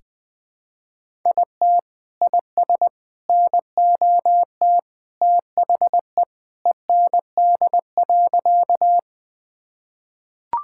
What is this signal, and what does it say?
Morse code, heard as a keyed tone, about 20 wpm, 700 Hz. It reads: IT IS NOT THE END.
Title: Sentences with 3-Word Phrases - 20wpm